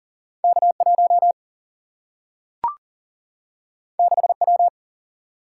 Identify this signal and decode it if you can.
Morse code, heard as a keyed tone, about 42 wpm, 700 Hz. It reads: K1 6W
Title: Single-Letter Single-Number 40wpm